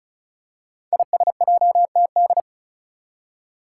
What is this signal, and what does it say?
Morse code, heard as a keyed tone, about 34 wpm, 700 Hz. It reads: ISJTD
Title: Five Challenging Letter-Number Combinations 35wpm